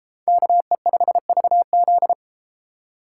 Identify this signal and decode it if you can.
Morse code, heard as a keyed tone, about 32 wpm, 700 Hz. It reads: KE5VZ